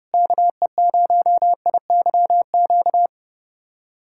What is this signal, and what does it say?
Morse code, heard as a keyed tone, about 30 wpm, 700 Hz. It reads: KE0IYQ
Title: USA Callsign DE Callsign - 6 Characters - 30wpm